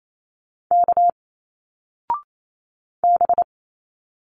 Morse code, heard as a keyed tone, about 28 wpm, 700 Hz. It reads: K B
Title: Single Letters 28wpm